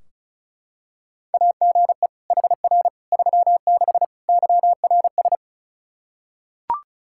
Morse code, heard as a keyed tone, about 36 wpm, 700 Hz. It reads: AGE HR 36 YRS
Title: Practice QSO - Exchange 3 - More Info - 35wpm